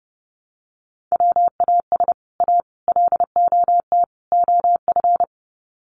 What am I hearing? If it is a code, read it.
Morse code, heard as a keyed tone, about 30 wpm, 700 Hz. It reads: WAS A LOT OF
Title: Sets of 4 Words 30wpm